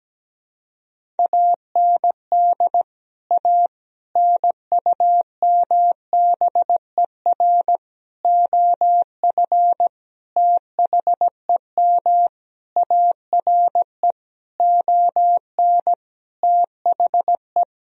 Morse code, heard as a keyed tone, about 17 wpm, 700 Hz. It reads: AND A NUMBER OF THEM ARE ON THE
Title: Sentences from Top 1000 Words - Review Part 1 - 17wpm